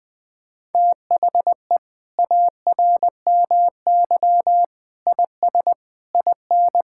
Morse code, heard as a keyed tone, about 21 wpm, 700 Hz. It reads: THE ARMY IS IN
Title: Sentences from Top 1000 Words - Review Part 6 - 20wpm